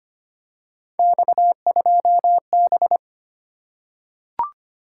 Morse code, heard as a keyed tone, about 25 wpm, 700 Hz. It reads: X2B
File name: Three Letter-Number Combinations - 25wpm